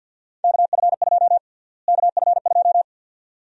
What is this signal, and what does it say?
Morse code, heard as a keyed tone, about 46 wpm, 700 Hz. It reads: KF2 KF2
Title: USA Call Sign Prefixes - Repeated 2x - 50wpm